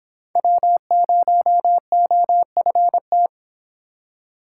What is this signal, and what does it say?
Morse code, heard as a keyed tone, about 25 wpm, 700 Hz. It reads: W0OFT